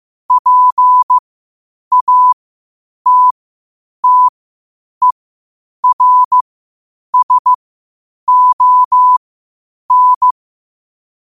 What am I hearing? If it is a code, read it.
Morse code, heard as a keyed tone, about 15 wpm, spaced as for 8 wpm, 1000 Hz. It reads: PATTERSON
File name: qotdfiles / 8wpm